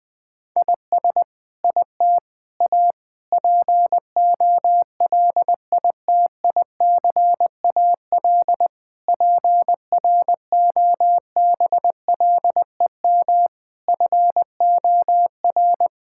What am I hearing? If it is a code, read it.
Morse code, heard as a keyed tone, about 20 wpm, 700 Hz. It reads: IS IT A POLITICAL PROBLEM FOR
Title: Sentences from Top 1000 Words - Review Part 5 - 20wpm